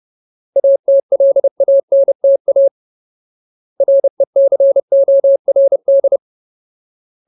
Morse code, heard as a keyed tone, about 30 wpm, 550 Hz. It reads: ATLANTA RECORD
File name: Ultimate Word List / 30WPM Ultimate